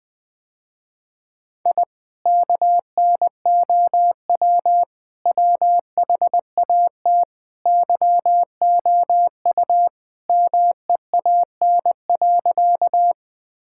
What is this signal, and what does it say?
Morse code, heard as a keyed tone, about 21 wpm, 700 Hz. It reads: I KNOW WHAT YOU MEAN.
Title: Sentences with 2-Word Phrases - Spoken Beginning - 20wpm